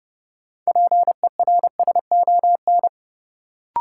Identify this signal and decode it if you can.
Morse code, heard as a keyed tone, about 30 wpm, 700 Hz. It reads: PERSON
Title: Sentences from Top 400 Words - New Word Drill - 30wpm